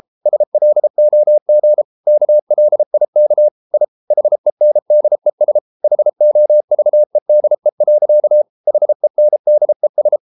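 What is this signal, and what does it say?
Morse code, heard as a keyed tone, about 33 wpm, 600 Hz. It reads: SLOG KLIK I HENDES HOVEDE. HENDES
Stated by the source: Morse code at 33 wpm, QRQ.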